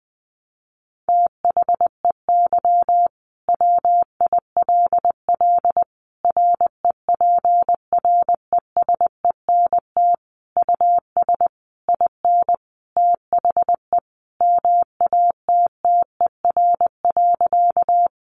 Morse code, heard as a keyed tone, about 20 wpm, 700 Hz. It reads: THEY WILL REPRESENT US IN THE MATTER.
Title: Sentences from Top 1200 Words 20wpm